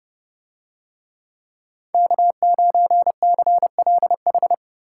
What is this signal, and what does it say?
Morse code, heard as a keyed tone, about 31 wpm, 700 Hz. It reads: K9CLH